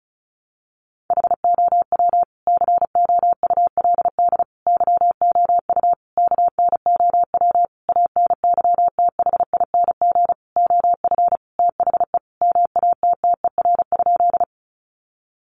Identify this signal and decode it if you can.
Morse code, heard as a keyed tone, about 34 wpm, 700 Hz. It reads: HOW COULD YOU KNOW ANYTHING OF THE MATTER?
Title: Sentences from Top 1000 Words 35wpm